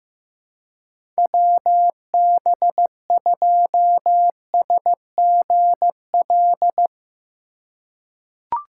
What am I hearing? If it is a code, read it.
Morse code, heard as a keyed tone, about 15 wpm, 700 Hz. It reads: WB2SGL
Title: USA Call Signs - 6 Characters - 2x Character Spacing - 15wpm